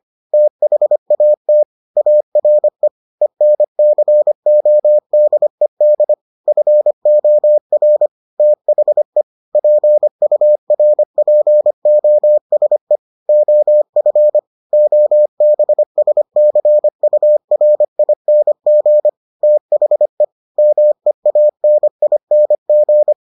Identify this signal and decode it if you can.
Morse code, heard as a keyed tone, about 25 wpm, 600 Hz. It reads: THAT ARE ENCODED FOR THE PURPOSE OF OBSCURING THE MEANING